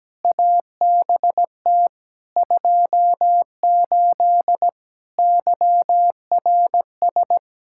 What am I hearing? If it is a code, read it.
Morse code, heard as a keyed tone, about 17 wpm, 700 Hz. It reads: ABT 28 YRS
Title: Practice QSO - Exchange 3 - More Info - 17wpm